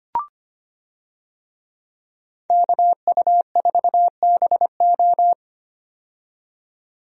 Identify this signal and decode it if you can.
Morse code, heard as a keyed tone, about 25 wpm, 700 Hz. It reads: KU4BO
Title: USA Call Signs - 5 Characters - 25wpm